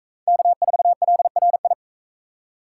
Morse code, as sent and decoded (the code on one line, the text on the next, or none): -.- ....- .-.. .-. ..
K4LRI